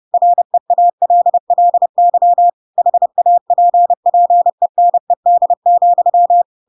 Morse code, heard as a keyed tone, about 30 wpm, 700 Hz. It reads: REALLY HAPPENED,
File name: Pinocchio / Pinocchio ch1 30 30 ik1hge